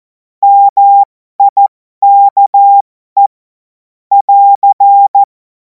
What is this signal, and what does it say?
Morse code, heard as a keyed tone, about 14 wpm, spaced as for 12 wpm, 800 Hz. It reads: MIKE <AR>
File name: Morsekurs / Witze